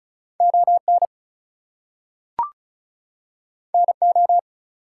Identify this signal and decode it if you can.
Morse code, heard as a keyed tone, about 36 wpm, 700 Hz. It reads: ON NO
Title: Two Letter Words - 35wpm